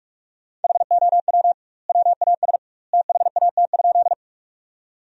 Morse code, heard as a keyed tone, about 46 wpm, 700 Hz. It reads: HOW WAS THAT?
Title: 2x Word-Spacing - Sentences from Top 100 Words - 45wpm